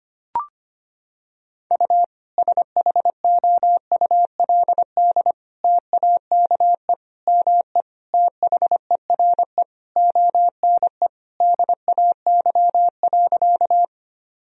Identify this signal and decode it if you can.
Morse code, heard as a keyed tone, about 25 wpm, 700 Hz. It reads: U SHOULD TAKE ME THERE ONE DAY.
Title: Sentences from Top 300 Words and Abbreviations 25wpm